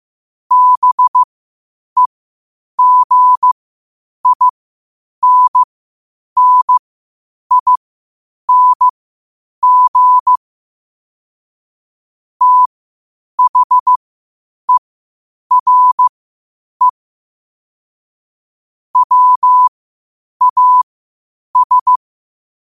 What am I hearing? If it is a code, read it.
Morse code, heard as a keyed tone, about 15 wpm, spaced as for 8 wpm, 1000 Hz. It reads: BEGINNING THERE WAS
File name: 8wpm